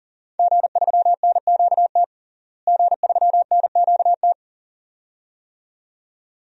Morse code, heard as a keyed tone, about 40 wpm, 700 Hz. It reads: G3NQT G3NQT
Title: DXCC-100 Call Signs - 5 Characters - Repeated 2x - 40wpm